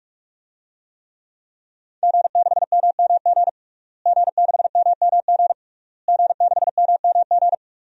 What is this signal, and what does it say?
Morse code, heard as a keyed tone, about 46 wpm, 700 Hz. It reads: G6MMG G6MMG G6MMG
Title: DXCC-100 Call Signs - 5 Characters - Repeated 3x - 45wpm